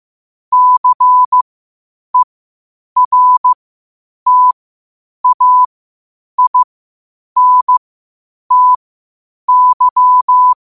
Morse code, heard as a keyed tone, about 15 wpm, spaced as for 8 wpm, 1000 Hz. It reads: CERTAINTY
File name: qotdfiles / 8wpm